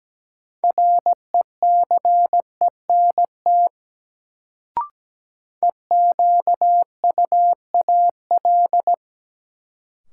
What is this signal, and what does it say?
Morse code, heard as a keyed tone, about 17 wpm, 700 Hz. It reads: RECENT EQUAL
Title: Sentences from Top 1400 Words - New Word Drill Alt - 17wpm